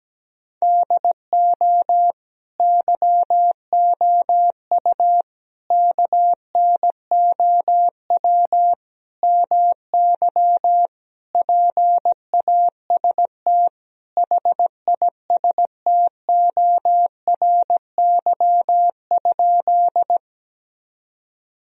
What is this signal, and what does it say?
Morse code, heard as a keyed tone, about 17 wpm, 700 Hz. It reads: DO YOU KNOW MY PAST HISTORY?
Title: Sentences from Top 1000 Words - Review Part 3 - 17wpm